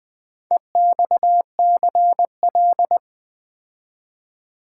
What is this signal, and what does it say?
Morse code, heard as a keyed tone, about 20 wpm, 700 Hz. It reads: EXCL
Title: Build A Word - 4 Letters At A Time - 11-Letter Words - 20wpm